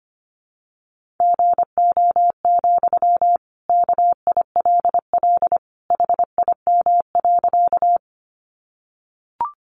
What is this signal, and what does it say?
Morse code, heard as a keyed tone, about 25 wpm, 700 Hz. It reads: GO, KILL HIM.
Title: Sentences from Top 1400 Words 25wpm